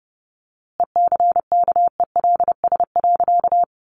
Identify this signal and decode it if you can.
Morse code, heard as a keyed tone, about 30 wpm, 700 Hz. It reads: ECKELS.
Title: A Sound Of Thunder 30wpm